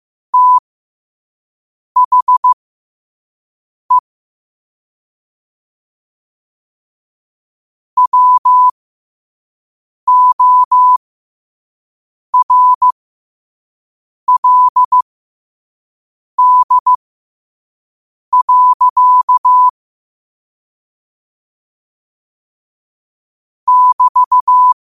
Morse code, heard as a keyed tone, about 15 wpm, spaced as for 5 wpm, 1000 Hz. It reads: THE WORLD. <BT>
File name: qotdfiles / 5wpm